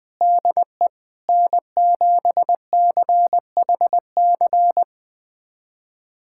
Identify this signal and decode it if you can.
Morse code, heard as a keyed tone, about 20 wpm, 700 Hz. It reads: DE N7CHC
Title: USA Callsign DE Callsign - 5 Characters - 20wpm